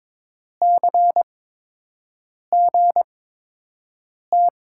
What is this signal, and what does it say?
Morse code, heard as a keyed tone, about 22 wpm, spaced as for 6 wpm, 700 Hz. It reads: CGT
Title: USA Call Signs - 5 Characters - 8x Character Spacing - 22wpm